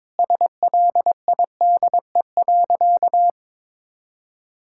Morse code, heard as a keyed tone, about 22 wpm, 700 Hz. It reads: SLIDE.
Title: Sentences from Top 1600 Words 22wpm